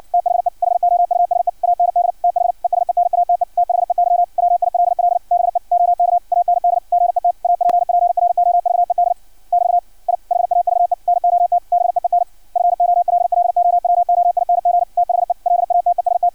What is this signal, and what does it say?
Morse code, heard as a keyed tone, about 56 wpm, 700 Hz. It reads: THE HOUSE AND THE SENATE THEY WILL BE OK AND GET APPROVED <BT> I HAVE NOT BEEN FOLLOWOING THE LATEST
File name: archived / CRSnet